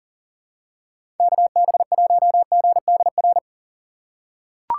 Morse code, heard as a keyed tone, about 40 wpm, 700 Hz. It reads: KB1GDR
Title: USA Call Signs - 6 Characters - 4x Character Spacing - 40wpm